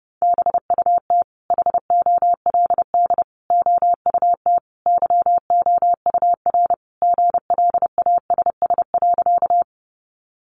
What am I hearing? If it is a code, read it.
Morse code, heard as a keyed tone, about 28 wpm, 700 Hz. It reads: BUT HOLD OUT YOUR GLASS.
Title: Sentences from Top 1300 Words 30wpm